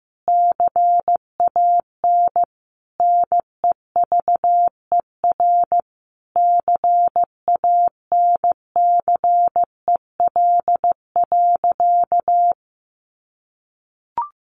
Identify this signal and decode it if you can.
Morse code, heard as a keyed tone, about 15 wpm, 700 Hz. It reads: CAN NEVER CANCEL.
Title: Sentences from Top 1200 Words 15wpm